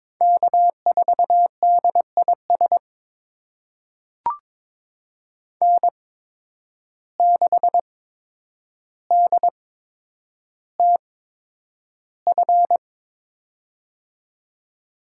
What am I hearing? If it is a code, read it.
Morse code, heard as a keyed tone, about 22 wpm, spaced as for 9 wpm, 700 Hz. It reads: K4DIS N6DTF
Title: USA Call Signs - 5 Characters - 8x Character Spacing - 22wpm